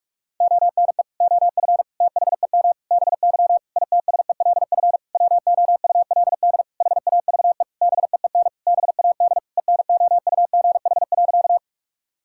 Morse code, heard as a keyed tone, about 46 wpm, 700 Hz. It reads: ONE OF THEM BY ITSELF WOULD HAVE BEEN BAD ENOUGH.
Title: Sentences from Top 1000 Words - Review Part 5 - 45wpm